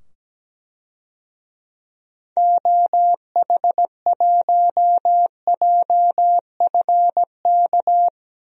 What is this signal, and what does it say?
Morse code, heard as a keyed tone, about 17 wpm, 700 Hz. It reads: OH1JFK
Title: DXCC-100 Call Signs - 6 Characters - 17wpm